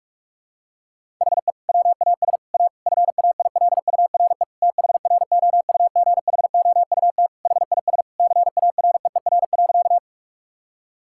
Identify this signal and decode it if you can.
Morse code, heard as a keyed tone, about 48 wpm, 700 Hz. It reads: HE WAS A FAILURE THROUGHOUT HIS CAREER.
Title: Sentences from Top 1000 Words - Review Part 6 - 45wpm